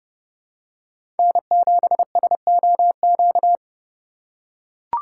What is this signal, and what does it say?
Morse code, heard as a keyed tone, about 30 wpm, 700 Hz. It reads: N7SOQ
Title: USA Call Signs - 8x Character Spacing - 30wpm